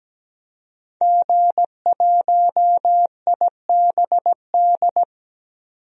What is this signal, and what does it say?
Morse code, heard as a keyed tone, about 17 wpm, 700 Hz. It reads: G1IBD
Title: DXCC-100 Call Signs - 5 Characters - 17wpm